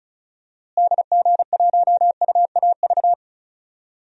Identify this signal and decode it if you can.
Morse code, heard as a keyed tone, about 36 wpm, 700 Hz. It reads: DG1UAV
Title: DXCC-100 Call Signs - 6 Characters - Repeated 2x - 35wpm